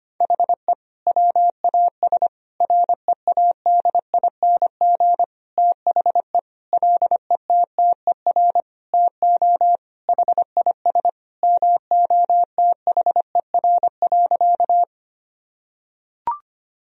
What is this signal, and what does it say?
Morse code, heard as a keyed tone, about 25 wpm, 700 Hz. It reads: HE WAS READING THE LETTER TO HIS MOTHER.
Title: Sentences from Top 1000 Words - Review Part 3 - 25wpm